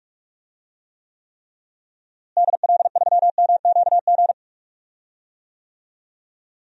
Morse code, heard as a keyed tone, about 46 wpm, 700 Hz. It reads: DL3MQG
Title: DXCC-100 Call Signs - 6 Characters - Repeated 3x - 45wpm